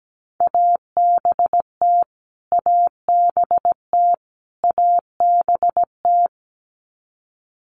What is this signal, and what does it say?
Morse code, heard as a keyed tone, about 17 wpm, 700 Hz. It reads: ABT ABT ABT
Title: Abbreviations Repeated 3x - 17wpm